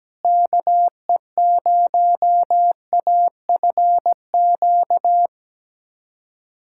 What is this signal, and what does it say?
Morse code, heard as a keyed tone, about 17 wpm, 700 Hz. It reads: KE0AFQ
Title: USA Callsign DE Callsign - 6 Characters - 17wpm